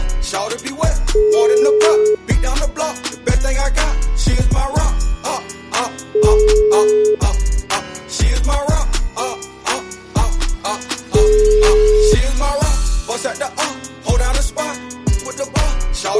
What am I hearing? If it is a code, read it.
Ringback tone (Europe, 425 Hz).